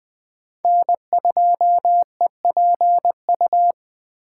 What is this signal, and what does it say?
Morse code, heard as a keyed tone, about 24 wpm, 700 Hz. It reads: N2EPU
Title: Speed Racing - USA 1x3 Call Signs 20wpm